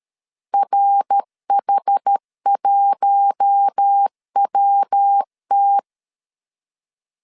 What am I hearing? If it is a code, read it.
Morse code, heard as a keyed tone, about 13 wpm, 800 Hz. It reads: RH1WT
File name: LEZIONI / 6 PKX0 / 6 PKX 12wpm 2024